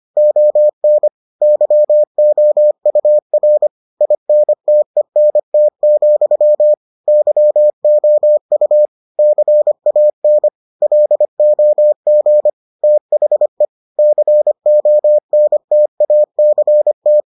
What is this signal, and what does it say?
Morse code, heard as a keyed tone, about 25 wpm, 600 Hz. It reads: ON YOUR INTENT, YOU CAN LOG THE CONTACT